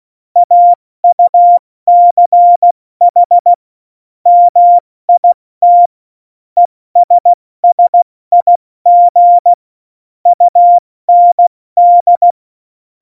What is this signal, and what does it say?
Morse code, heard as a keyed tone, about 16 wpm, 700 Hz. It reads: AUCH MIT ESSIG UND
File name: Morsekurs / Witze / Witz 13. Februar 2018, 70 F80